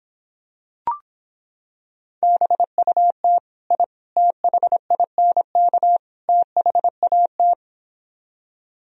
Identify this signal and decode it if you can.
Morse code, heard as a keyed tone, about 26 wpm, 700 Hz. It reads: BUT I THINK THAT